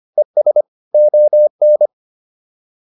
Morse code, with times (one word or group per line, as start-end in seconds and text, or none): 0.2-0.6 ES
0.9-1.9 ON